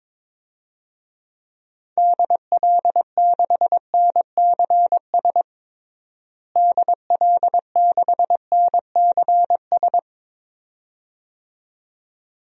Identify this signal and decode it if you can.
Morse code, heard as a keyed tone, about 22 wpm, 700 Hz. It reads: DL6NCS DL6NCS
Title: DXCC-100 Call Signs - 6 Characters - Repeated 2x - 22wpm